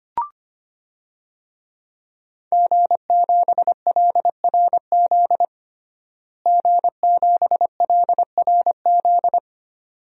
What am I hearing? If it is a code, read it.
Morse code, heard as a keyed tone, about 25 wpm, 700 Hz. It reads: G7LRZ G7LRZ
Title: DXCC-100 Call Signs - 5 Characters - Repeated 2x - 25wpm